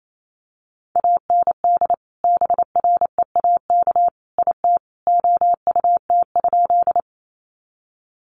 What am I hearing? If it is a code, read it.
Morse code, heard as a keyed tone, about 28 wpm, 700 Hz. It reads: AND BREAK IT OUT?
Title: Sentences from Top 900 Words 28wpm